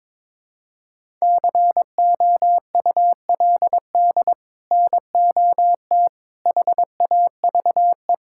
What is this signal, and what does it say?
Morse code, heard as a keyed tone, about 22 wpm, 700 Hz. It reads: COULD NOT HAVE
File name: practice / 22wpm / Sets of 8 Words - 22wpm